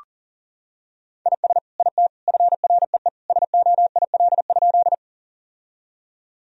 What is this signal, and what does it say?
Morse code, heard as a keyed tone, about 37 wpm, 700 Hz. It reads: IS IT FREE SOIL?
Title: Sentences from Top 1000 Words - Review Part 4 - 40wpm